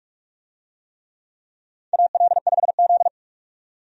Morse code, heard as a keyed tone, about 46 wpm, 700 Hz. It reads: AL5Z
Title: USA Call Signs - 4 Characters - 45wpm